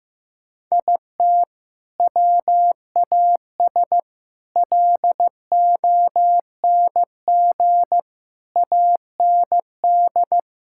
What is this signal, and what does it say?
Morse code, heard as a keyed tone, about 16 wpm, 700 Hz. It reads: IT WAS LONG AND
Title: Sentences from Top 1600 Words 15wpm